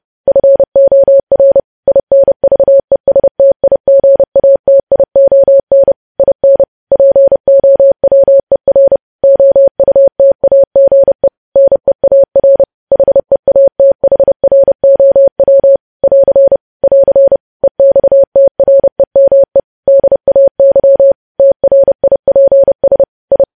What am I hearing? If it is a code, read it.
Morse code, heard as a keyed tone, about 30 wpm, 550 Hz. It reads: FOR INVESTIGATION IN POWER OUTAGE NEAR HEATHROW <AR> <AR> EXTREME DAY TRIPS I
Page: news headlines in morse code at 30 wpm